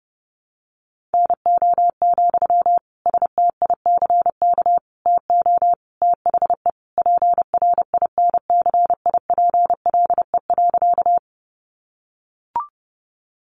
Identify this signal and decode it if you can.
Morse code, heard as a keyed tone, about 30 wpm, 700 Hz. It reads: NO, STICK TO THE PRINCIPLE.
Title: Sentences from Top 1400 Words 30wpm